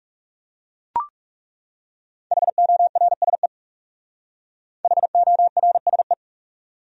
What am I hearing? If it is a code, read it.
Morse code, heard as a keyed tone, about 37 wpm, 700 Hz. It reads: HORSE HORSE
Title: Speed-Racing Top 500 Words 30wpm